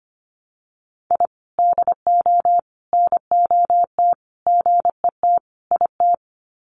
Morse code, heard as a keyed tone, about 25 wpm, 700 Hz. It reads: I DO NOT GET IT
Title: Sets of 5 Words Encore 25wpm